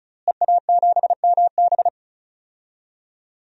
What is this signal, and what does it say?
Morse code, heard as a keyed tone, about 36 wpm, 700 Hz. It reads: EA7MB
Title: DXCC-100 Call Signs - 5 Characters - Repeated 2x - 35wpm